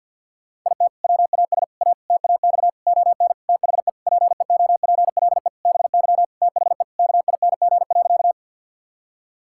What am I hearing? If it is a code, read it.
Morse code, heard as a keyed tone, about 50 wpm, 700 Hz. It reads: IT WAS A TAX ON THE PEOPLE BY THE KING.
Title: Sentences from Top 1000 Words - Review Part 5 - 50wpm